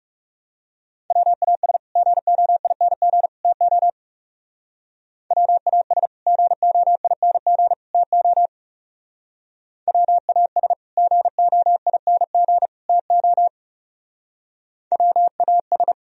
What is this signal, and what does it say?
Morse code, heard as a keyed tone, about 38 wpm, 700 Hz. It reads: WAS GOING TO WAS GOING TO WAS GOING TO WAS